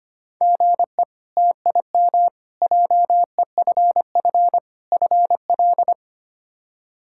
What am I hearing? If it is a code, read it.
Morse code, heard as a keyed tone, about 25 wpm, 700 Hz. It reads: GE TIM JEFF FL
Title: K1USN SST QSO Practice - 25wpm